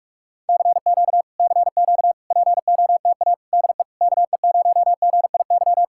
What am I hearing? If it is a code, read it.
Morse code, heard as a keyed tone, about 50 wpm, 700 Hz. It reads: CQ CQ POTA DE KE0GIY